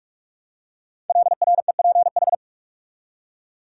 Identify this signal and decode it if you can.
Morse code, heard as a keyed tone, about 44 wpm, 700 Hz. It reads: LREPH
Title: Code Groups - 5 Letters - 45wpm